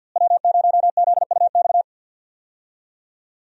Morse code, heard as a keyed tone, about 49 wpm, 700 Hz. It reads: W0ZUX
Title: USA Call Signs - 5 Characters - Repeated 2x - 50wpm